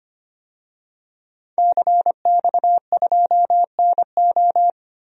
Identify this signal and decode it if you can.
Morse code, heard as a keyed tone, about 25 wpm, 700 Hz. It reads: CX2NO
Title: DXCC-100 Call Signs - 5 Characters - 25wpm